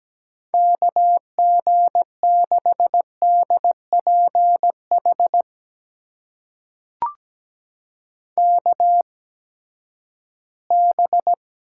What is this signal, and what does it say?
Morse code, heard as a keyed tone, about 17 wpm, spaced as for 12 wpm, 700 Hz. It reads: KG6DPH KB
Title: USA Call Signs - 6 Characters - 8x Character Spacing - 17wpm